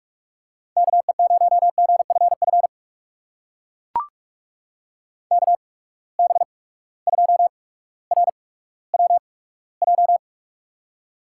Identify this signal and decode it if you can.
Morse code, heard as a keyed tone, about 42 wpm, spaced as for 19 wpm, 700 Hz. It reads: KE0GFF KB2RWJ
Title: USA Call Signs - 6 Characters - 8x Character Spacing - 45wpm